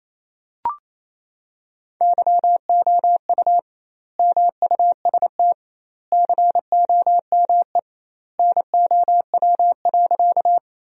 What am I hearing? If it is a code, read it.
Morse code, heard as a keyed tone, about 28 wpm, 700 Hz. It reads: YOU MUST COME NOW.
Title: 2x Word-Spacing - Sentences from Top 200 Words - 28wpm